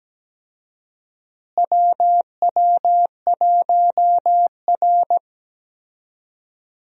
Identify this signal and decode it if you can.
Morse code, heard as a keyed tone, about 17 wpm, 700 Hz. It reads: WW1R
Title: USA Call Signs - 4 Characters - Repeated 2x - 17wpm